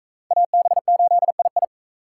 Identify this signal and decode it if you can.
Morse code, heard as a keyed tone, about 39 wpm, 700 Hz. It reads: AB8II